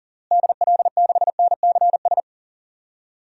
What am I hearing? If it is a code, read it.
Morse code, heard as a keyed tone, about 38 wpm, 700 Hz. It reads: DL6NCS
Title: DXCC-100 Call Signs - 6 Characters - Repeated 2x - 40wpm